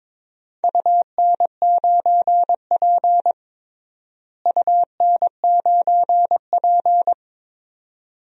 Morse code, heard as a keyed tone, about 21 wpm, 700 Hz. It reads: UN9P UN9P
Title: DXCC-100 Call Signs - 4 Characters - Repeated 2x - 22wpm